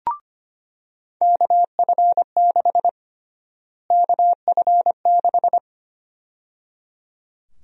Morse code, heard as a keyed tone, about 25 wpm, 700 Hz. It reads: KF6 KF6
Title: USA Call Sign Prefixes - Repeated 2x - 25wpm